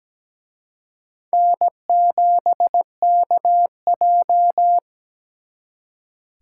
Morse code, heard as a keyed tone, about 20 wpm, 700 Hz. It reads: N7KJ
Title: Speed Racing - USA Call Signs 17wpm